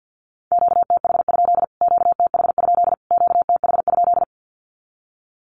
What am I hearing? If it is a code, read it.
Morse code, heard as a keyed tone, about 50 wpm, 700 Hz. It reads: QTH? QTH? QTH?
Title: Q-Codes Repeated 3x 50wpm